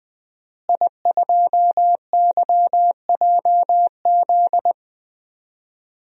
Morse code, heard as a keyed tone, about 20 wpm, 700 Hz. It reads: I2YJZ
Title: DXCC-100 Call Signs - 5 Characters - Repeated 3x - 20wpm